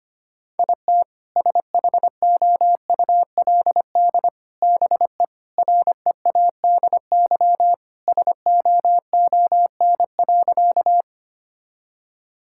Morse code, heard as a keyed tone, about 25 wpm, 700 Hz. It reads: IT SHOULD BE READY SOON.
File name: practice / 25wpm / Sentences with 2-Word Phrases - 25wpm